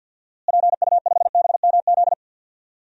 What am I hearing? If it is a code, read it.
Morse code, heard as a keyed tone, about 52 wpm, 700 Hz. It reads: PU5BMZ
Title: DXCC-100 Call Signs - 6 Characters - Repeated 3x - 50wpm